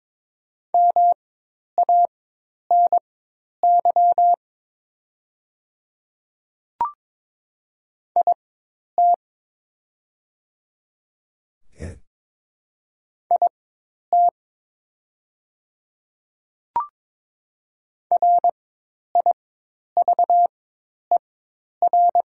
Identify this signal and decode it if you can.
Morse code, heard as a keyed tone, about 22 wpm, spaced as for 10 wpm, 700 Hz. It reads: MANY IT IT RIVER
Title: 4x Character Spacing - Top 300 Words - 22wpm